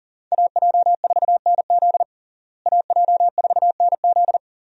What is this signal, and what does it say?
Morse code, heard as a keyed tone, about 37 wpm, 700 Hz. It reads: AJ4NZ AJ4NZ
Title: USA Call Signs - 5 Characters - Repeated 2x - 40wpm